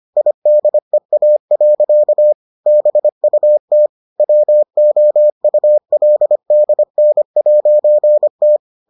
Morse code, heard as a keyed tone, about 25 wpm, 600 Hz. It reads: IDEA. BUT WOULDN'T